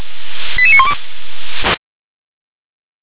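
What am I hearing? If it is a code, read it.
Five-tone selective call (ZVEI), heard as quick runs of five tones, tones of about 70 ms. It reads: 88912
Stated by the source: ZVEI five-tone selective call, received on SDR.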